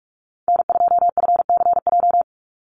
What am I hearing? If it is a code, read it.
Morse code, heard as a keyed tone, about 44 wpm, 700 Hz. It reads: N2FCJ
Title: USA Call Signs 45wpm